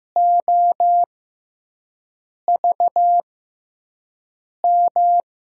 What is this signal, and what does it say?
Morse code, heard as a keyed tone, about 15 wpm, spaced as for 5 wpm, 700 Hz. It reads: OVM
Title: USA Call Signs - 6 Characters - 6x Character Spacing - 15wpm